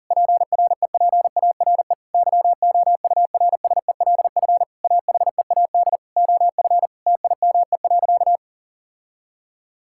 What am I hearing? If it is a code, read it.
Morse code, heard as a keyed tone, about 40 wpm, 700 Hz. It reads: PREPARE YOURSELF AHEAD OF TIME.
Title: Sentences from Top 1000 Words - Review Part 6 - 40wpm